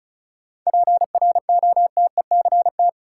Morse code, heard as a keyed tone, about 34 wpm, 700 Hz. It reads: PROTECT
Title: Sentences from Top 900 Words - New Word Drill - 35wpm